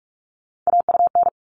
Morse code, heard as a keyed tone, about 44 wpm, 700 Hz. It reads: AUN
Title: All Cut Numbers - 3-Digit 45wpm